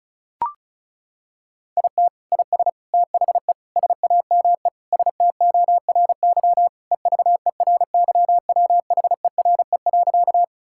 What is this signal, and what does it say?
Morse code, heard as a keyed tone, about 36 wpm, 700 Hz. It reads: IT IS THE SAME STORY EVERYWHERE.
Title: Sentences from Top 1600 Words 35wpm